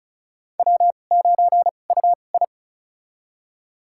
Morse code, heard as a keyed tone, about 36 wpm, spaced as for 25 wpm, 700 Hz. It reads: W9UI
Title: USA Call Signs - 4 Characters - 2x Character Spacing - 35wpm